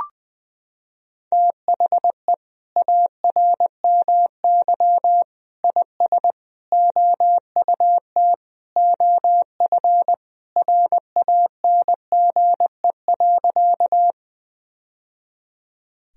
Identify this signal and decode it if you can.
Morse code, heard as a keyed tone, about 20 wpm, 700 Hz. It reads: THE ARMY IS OUT OF RANGE.
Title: Sentences from Top 1000 Words - Review Part 4 - 20wpm